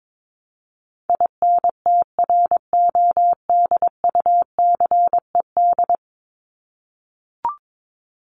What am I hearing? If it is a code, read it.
Morse code, heard as a keyed tone, about 22 wpm, 700 Hz. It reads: INTRODUCED
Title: ED Suffix - Top 100 Words with Base Words 22wpm